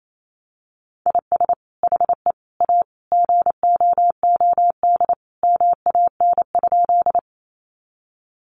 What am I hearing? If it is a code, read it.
Morse code, heard as a keyed tone, about 28 wpm, 700 Hz. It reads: IS HE A GOOD MAN?
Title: Sentences from Top 200 Words 28wpm